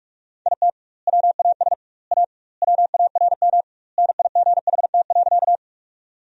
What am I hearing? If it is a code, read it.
Morse code, heard as a keyed tone, about 46 wpm, 700 Hz. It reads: IT WAS A WARM NIGHT.
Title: 2x Word-Spacing - Sentences from Top 500 Words - 45wpm